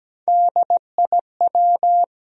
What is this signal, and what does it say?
Morse code, heard as a keyed tone, about 17 wpm, 700 Hz. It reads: DIW
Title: USA Call Sign Postfixes - 3 Letters 17wpm